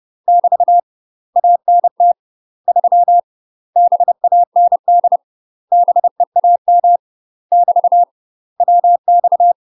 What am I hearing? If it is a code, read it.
Morse code, heard as a keyed tone, about 30 wpm, 700 Hz. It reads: <BT> ANT 3 BAND BEAM <BT> WX
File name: UK MASTER QSOs TRAINING / 30WPM / UK MASTER QSOs 0004